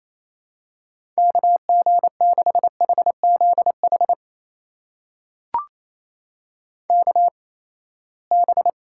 Morse code, heard as a keyed tone, about 28 wpm, spaced as for 19 wpm, 700 Hz. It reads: KG6HZH KB
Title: USA Call Signs - 6 Characters - 8x Character Spacing - 28wpm